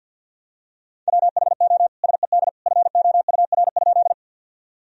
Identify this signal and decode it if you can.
Morse code, heard as a keyed tone, about 50 wpm, 700 Hz. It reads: WHO SED FOUR?
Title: Sentences from Top 300 Words and Abbreviations 50wpm